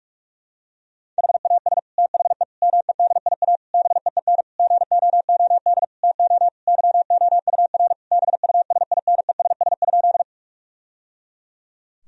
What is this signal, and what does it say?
Morse code, heard as a keyed tone, about 46 wpm, 700 Hz. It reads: HAS THE MEDIA BEEN GOOD TO YOUR BUSINESS?
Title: Sentences from Top 1000 Words - Review Part 4 - 45wpm